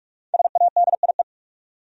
Morse code, heard as a keyed tone, about 46 wpm, 700 Hz. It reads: SADIE
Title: USA Common Names - 45wpm